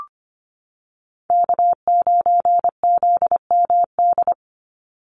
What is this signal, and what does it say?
Morse code, heard as a keyed tone, about 25 wpm, 700 Hz. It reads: K9ZMD
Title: NA SOTA Chaser Callsigns 25wpm